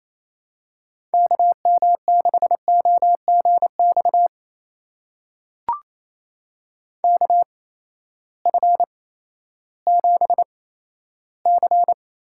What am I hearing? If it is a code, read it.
Morse code, heard as a keyed tone, about 28 wpm, spaced as for 14 wpm, 700 Hz. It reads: KM6OGX KF7C
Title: USA Call Signs - 6 Characters - 8x Character Spacing - 28wpm